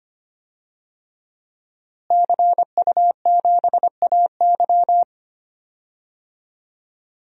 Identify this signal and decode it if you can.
Morse code, heard as a keyed tone, about 25 wpm, 700 Hz. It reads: CU7AY
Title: DXCC-100 Call Signs - 5 Characters - Repeated 3x - 25wpm